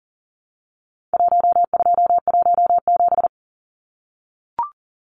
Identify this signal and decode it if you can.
Morse code, heard as a keyed tone, about 40 wpm, 700 Hz. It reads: 1217
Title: Numbers - Contest Style - 4-Digit 40wpm